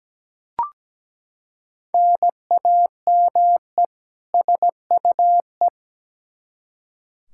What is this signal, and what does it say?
Morse code, heard as a keyed tone, about 17 wpm, 700 Hz. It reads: NAME SUE
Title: Practice QSO - Exchange 1 - Beginning - 17wpm